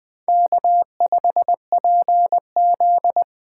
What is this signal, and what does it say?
Morse code, heard as a keyed tone, about 24 wpm, 700 Hz. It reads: K5PZ
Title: Speed Racing - USA 1x2 Call Signs 20wpm